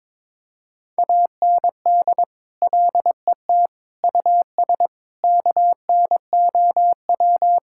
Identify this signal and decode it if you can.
Morse code, heard as a keyed tone, about 22 wpm, 700 Hz. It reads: AND LET US KNOW
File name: practice / 22wpm / Sets of 7 Words - 22wpm